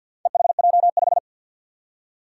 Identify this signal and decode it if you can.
Morse code, heard as a keyed tone, about 51 wpm, 700 Hz. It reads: EHJ5
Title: Four Challenging Letter-Number Combinations 50wpm